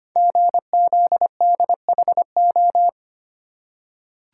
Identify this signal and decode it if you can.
Morse code, heard as a keyed tone, about 25 wpm, 700 Hz. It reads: GZDHO
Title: Code Groups - 5 Letters - 25wpm